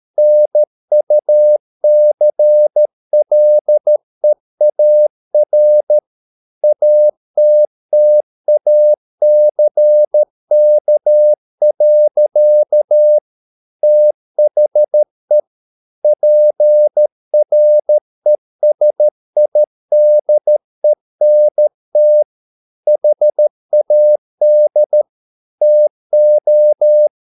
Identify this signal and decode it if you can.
Morse code, heard as a keyed tone, about 13 wpm, 600 Hz. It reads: NUCLEAR ATTACK. THE PRESIDENT HAD TO